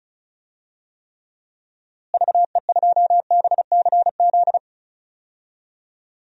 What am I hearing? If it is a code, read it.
Morse code, heard as a keyed tone, about 34 wpm, 700 Hz. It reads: VE2BCZ
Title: DXCC-100 Call Signs - 6 Characters - Repeated 2x - 35wpm